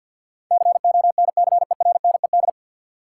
Morse code, heard as a keyed tone, about 53 wpm, 700 Hz. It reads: CONCERNED
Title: Sentences from Top 1000 Words - Review Part 6 - New Word Drill - 50wpm